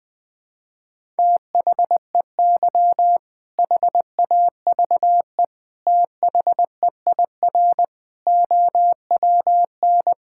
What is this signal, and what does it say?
Morse code, heard as a keyed tone, about 21 wpm, 700 Hz. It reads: THEY HAVE THEIR OWN
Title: All-Age Appropriate Jokes - 20wpm